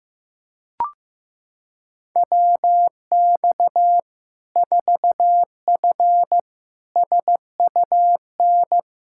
Morse code, heard as a keyed tone, about 15 wpm, 700 Hz. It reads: WX 4F SUN
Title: Practice QSO - Exchange 2 - Info - 15wpm